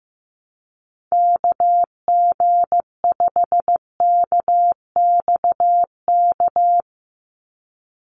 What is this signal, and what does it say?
Morse code, heard as a keyed tone, about 15 wpm, 700 Hz. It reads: KG5KXK
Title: USA Call Signs 15wpm